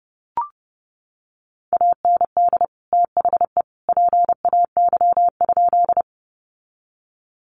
Morse code, heard as a keyed tone, about 30 wpm, 700 Hz. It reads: AND THE PAY?
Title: Sentences from Top 700 Words 30wpm